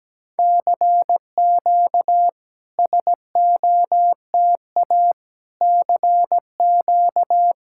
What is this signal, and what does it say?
Morse code, heard as a keyed tone, about 16 wpm, 700 Hz. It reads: CQ SOTA CQ
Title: SOTA QSO Practice 17wpm